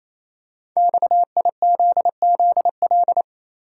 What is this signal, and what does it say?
Morse code, heard as a keyed tone, about 27 wpm, 700 Hz. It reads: XIZZL
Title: Code Groups - 5 Letters - 28wpm